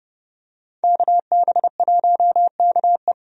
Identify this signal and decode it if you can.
Morse code, heard as a keyed tone, about 30 wpm, 700 Hz. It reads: KB1KE
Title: USA Call Signs - 5 Characters - 30wpm